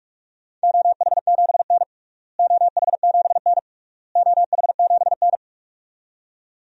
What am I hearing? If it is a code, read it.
Morse code, heard as a keyed tone, about 46 wpm, 700 Hz. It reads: OH7N OH7N OH7N
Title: DXCC-100 Call Signs - 4 Characters - Repeated 3x - 45wpm